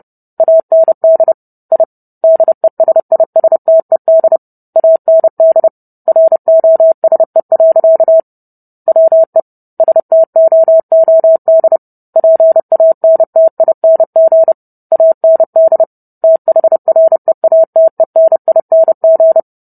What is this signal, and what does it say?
Morse code, heard as a keyed tone, about 30 wpm, 650 Hz. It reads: AND I DESISTED AND ROSE. WE STOOD PANTING AND THREATENING